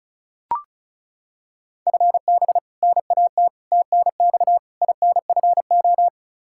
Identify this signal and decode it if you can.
Morse code, heard as a keyed tone, about 34 wpm, 700 Hz. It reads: FB NAT TNX INFO
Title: Practice QSO - Exchange 3 - More Info - 35wpm